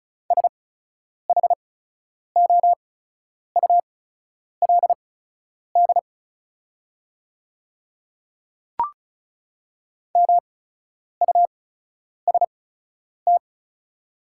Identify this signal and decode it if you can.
Morse code, heard as a keyed tone, about 34 wpm, spaced as for 10 wpm, 700 Hz. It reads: SHOULD MUST
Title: 8x Character Spacing - Top 300 Words - 35wpm